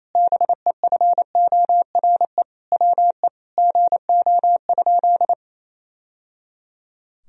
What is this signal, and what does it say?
Morse code, heard as a keyed tone, about 28 wpm, 700 Hz. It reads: BEFORE WE GO?
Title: Sentences from Top 1000 Words - Review Part 2 - 28wpm